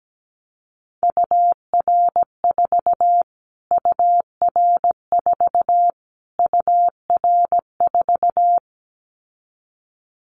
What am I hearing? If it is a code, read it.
Morse code, heard as a keyed tone, about 17 wpm, 700 Hz. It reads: UR4 UR4 UR4
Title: Callsign Prefixes for Top 25 DXCC - Repeated 3x - 17wpm